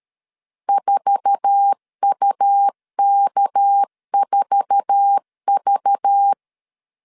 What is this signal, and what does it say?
Morse code, heard as a keyed tone, about 13 wpm, 800 Hz. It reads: 4UK4V
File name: LEZIONI / 8 YJ45 / 8 YJ4 12wpm 2151